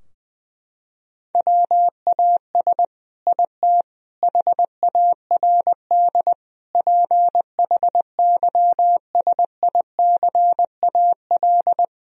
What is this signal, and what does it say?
Morse code, heard as a keyed tone, about 20 wpm, 700 Hz. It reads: WAS IT HARD PHYSICAL
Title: Sentences from Top 1000 Words - Review Part 4 - 20wpm